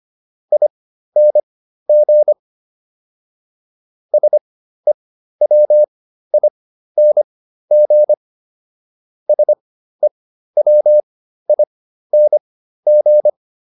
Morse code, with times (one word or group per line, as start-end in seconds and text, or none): 0.5-2.3 ING
4.1-8.1 SEWING
9.3-13.3 SEWING